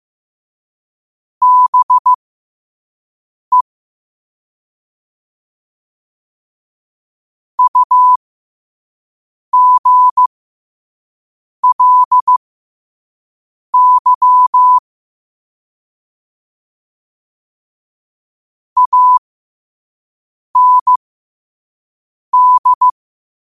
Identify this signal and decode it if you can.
Morse code, heard as a keyed tone, about 15 wpm, spaced as for 5 wpm, 1000 Hz. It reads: BE UGLY AND
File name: qotdfiles / 5wpm